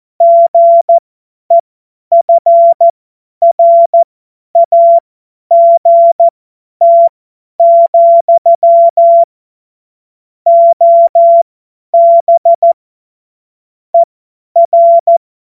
Morse code, heard as a keyed tone, about 14 wpm, spaced as for 10 wpm, 680 Hz. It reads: GEFRAGT, OB ER
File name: Morsekurs / Witze / Witz 07.November 2023 50 F70